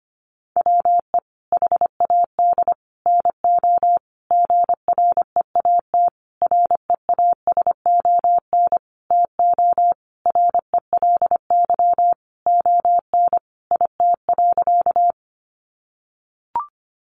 Morse code, heard as a keyed tone, about 25 wpm, 700 Hz. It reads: WE HAD NO GREAT REASON TO RELY ON IT.
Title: Sentences from Top 1200 Words 25wpm